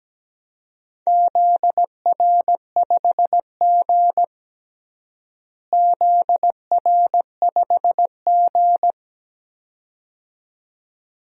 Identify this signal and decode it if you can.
Morse code, heard as a keyed tone, about 17 wpm, 700 Hz. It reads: ZR5G ZR5G
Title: DXCC-100 Call Signs - 4 Characters - Repeated 2x - 17wpm